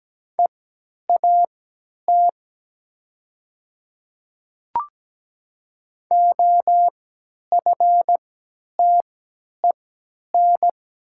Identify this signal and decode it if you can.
Morse code, heard as a keyed tone, about 17 wpm, spaced as for 10 wpm, 700 Hz. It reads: EAT OFTEN
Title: 3x Character Spacing - Top 300 Words - 17wpm